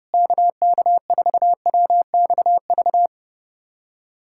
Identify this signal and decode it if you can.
Morse code, heard as a keyed tone, about 32 wpm, 700 Hz. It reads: KK4WXV
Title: USA Callsign DE Callsign - 6 Characters - 30wpm